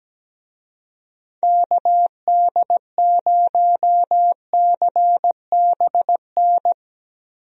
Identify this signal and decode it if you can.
Morse code, heard as a keyed tone, about 17 wpm, 700 Hz. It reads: KD0CBN
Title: USA Call Signs - 6 Characters - Repeated 3x - 17wpm